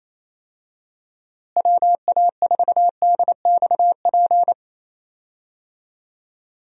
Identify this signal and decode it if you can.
Morse code, heard as a keyed tone, about 27 wpm, 700 Hz. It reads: WA4DXP
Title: USA Call Signs - 6 Characters - Repeated 3x - 28wpm